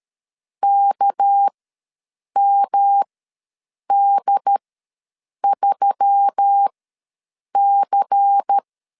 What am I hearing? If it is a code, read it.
Morse code, heard as a keyed tone, about 13 wpm, spaced as for 7 wpm, 800 Hz. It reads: KMD3C
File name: LEZIONI / 8 YJ45 / 8 Y ws 12wpm 3409